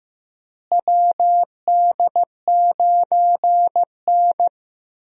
Morse code, heard as a keyed tone, about 15 wpm, 700 Hz. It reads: WD9N
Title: USA Call Signs - 4 Characters - Repeated 3x - 15wpm